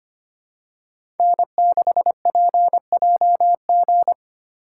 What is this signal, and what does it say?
Morse code, heard as a keyed tone, about 25 wpm, 700 Hz. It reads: N6PJG
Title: USA Call Signs - 5 Characters - 3x Character Spacing - 25wpm